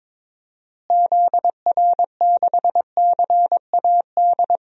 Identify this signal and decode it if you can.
Morse code, heard as a keyed tone, about 22 wpm, 700 Hz. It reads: ZR6CAD
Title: DXCC-100 Call Signs - 6 Characters - Repeated 2x - 22wpm